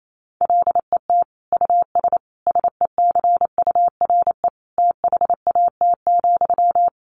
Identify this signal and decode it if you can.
Morse code, heard as a keyed tone, about 28 wpm, 700 Hz. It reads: LET US SECURE THAT,
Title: Sentences from Top 1200 Words 28wpm